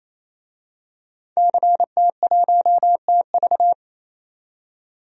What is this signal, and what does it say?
Morse code, heard as a keyed tone, about 29 wpm, 690 Hz. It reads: CT1TV
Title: DXCC-100 Call Signs - 5 Characters - 28wpm